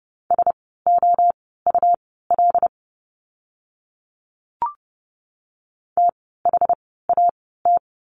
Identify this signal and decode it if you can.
Morse code, heard as a keyed tone, about 30 wpm, spaced as for 17 wpm, 700 Hz. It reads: SOUL THAT
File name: Lesson 17f - Letter u - Words 30wpm